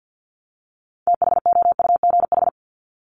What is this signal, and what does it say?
Morse code, heard as a keyed tone, about 50 wpm, 700 Hz. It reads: THOUGH
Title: Top 500 Words 50wpm